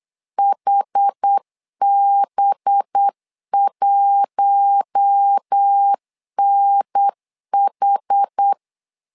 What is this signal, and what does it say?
Morse code, heard as a keyed tone, about 8 wpm, 800 Hz. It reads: HB1NH
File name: LEZIONI / 5 LBVC / 5 LB 8wpm 2816